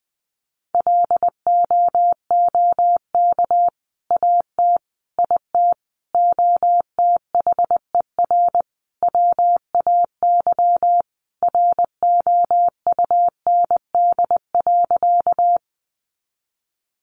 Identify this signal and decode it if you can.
Morse code, heard as a keyed tone, about 20 wpm, 700 Hz. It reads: LOOK AT IT OTHER WAY ROUND.
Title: Sentences from Top 200 Words 20wpm